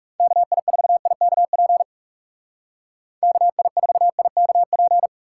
Morse code, heard as a keyed tone, about 36 wpm, 700 Hz. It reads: KI4IKP KI4IKP
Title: Speed Racing - USA 2x3 Call Signs 30wpm